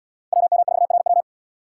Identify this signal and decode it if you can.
Morse code, heard as a keyed tone, about 76 wpm, 700 Hz. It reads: WM4RW